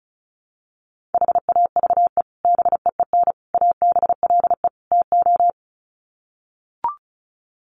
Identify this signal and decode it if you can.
Morse code, heard as a keyed tone, about 34 wpm, 700 Hz. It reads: HAVE BEEN ABLE TO
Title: Sets of 4 Words 35wpm